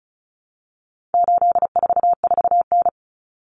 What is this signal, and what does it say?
Morse code, heard as a keyed tone, about 34 wpm, 700 Hz. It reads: 844N
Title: Common Cut Numbers - 4-Digit 35wpm